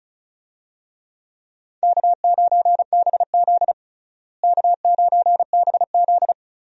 Morse code, heard as a keyed tone, about 36 wpm, 700 Hz. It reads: K9BZ K9BZ
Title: USA Call Signs - 4 Characters - Repeated 3x - 35wpm